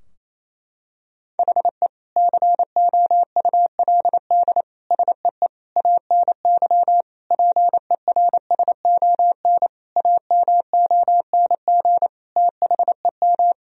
Morse code, heard as a keyed tone, about 28 wpm, 700 Hz. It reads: HE COULD SEE ANY PERSON AMONG THEM
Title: Sentences from Top 1000 Words - Review Part 3 - 28wpm